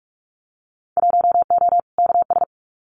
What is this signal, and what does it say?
Morse code, heard as a keyed tone, about 46 wpm, 700 Hz. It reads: 1O KS
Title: Winter Field Day Exchanges 45wpm